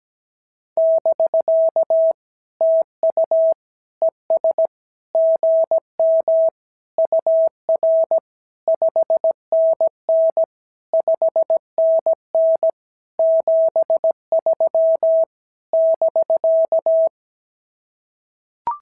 Morse code, heard as a keyed tone, about 17 wpm, 650 Hz. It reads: <BK> TU ES GM UR 5NN 5NN 73 <BK>
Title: SOTA QSO Practice 17wpm